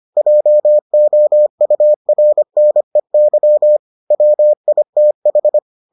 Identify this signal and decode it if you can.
Morse code, heard as a keyed tone, about 25 wpm, 600 Hz. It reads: JOURNEY WITH